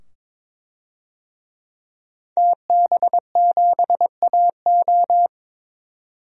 Morse code, heard as a keyed tone, about 22 wpm, 700 Hz. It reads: TB7AO
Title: DXCC-100 Call Signs - 5 Characters - 22wpm